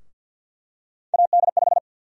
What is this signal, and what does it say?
Morse code, heard as a keyed tone, about 52 wpm, 700 Hz. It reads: AD5
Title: USA Call Sign Prefixes - 50wpm